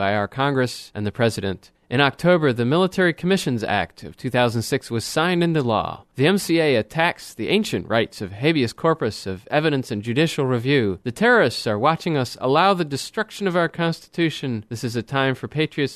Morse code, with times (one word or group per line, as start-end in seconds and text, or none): none